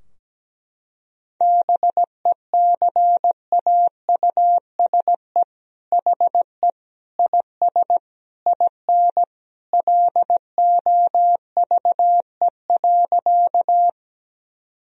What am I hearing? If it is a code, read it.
Morse code, heard as a keyed tone, about 17 wpm, 700 Hz. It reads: BECAUSE HE IS IN LOVE.
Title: Sentences from Top 1000 Words - Review Part 3 - 17wpm